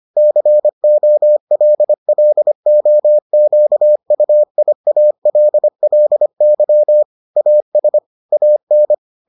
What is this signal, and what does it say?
Morse code, heard as a keyed tone, about 25 wpm, 600 Hz. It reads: COLLOQUIALLY AS AN